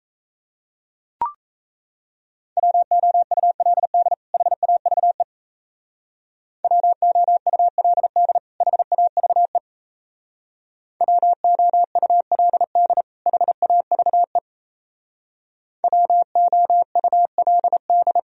Morse code, heard as a keyed tone, about 35 wpm, 700 Hz. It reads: WOULD HAVE WOULD HAVE WOULD HAVE WOULD